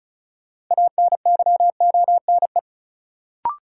Morse code, heard as a keyed tone, about 36 wpm, 700 Hz. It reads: ANYONE
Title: Sets of 5 Words Encore - New Word Drill - 35wpm